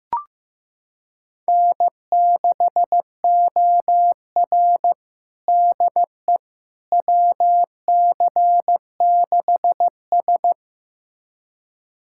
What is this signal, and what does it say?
Morse code, heard as a keyed tone, about 16 wpm, 700 Hz. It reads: N6OR DE WC6S
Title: USA Callsign DE Callsign - 4 Characters - 15wpm